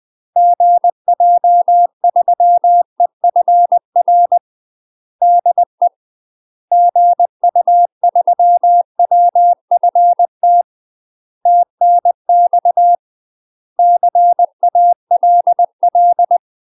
Morse code, heard as a keyed tone, about 20 wpm, 700 Hz. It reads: GJ3EFR DE GU3WFT TNX CALL
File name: UK MASTER QSOs TRAINING / 20WPM / UK MASTER QSOs 0030